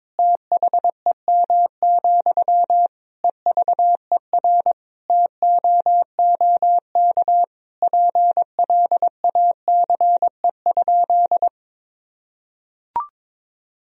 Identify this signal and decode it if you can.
Morse code, heard as a keyed tone, about 22 wpm, 700 Hz. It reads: THEM, EVER TOOK PLACE?
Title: Sentences from Top 1000 Words - Review Part 1 - 22wpm